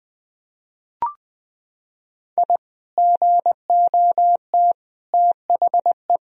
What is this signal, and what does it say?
Morse code, heard as a keyed tone, about 20 wpm, 700 Hz. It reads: I GOT THE
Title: Sentences from Top 1000 Words - Review Part 2 - 20wpm